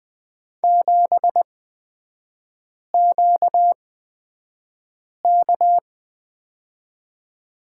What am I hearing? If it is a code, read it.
Morse code, heard as a keyed tone, about 20 wpm, spaced as for 5 wpm, 700 Hz. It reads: 7QK
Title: 236 Lesson 38e - Letter q - Three Characters 20wpm